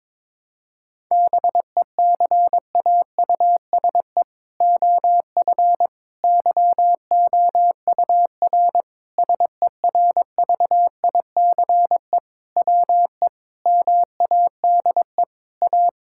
Morse code, heard as a keyed tone, about 22 wpm, 700 Hz. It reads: BECAUSE OF YOUR SERVICE WE MADE A
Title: Sentences from Top 1000 Words - Review Part 5 - 22wpm